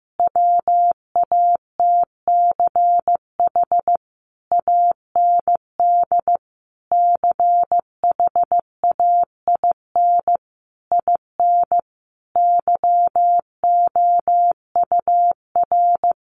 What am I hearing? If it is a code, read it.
Morse code, heard as a keyed tone, about 15 wpm, 700 Hz. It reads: WATCH AND CHAIN IN YOUR
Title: Sentences from Top 1500 Words 15wpm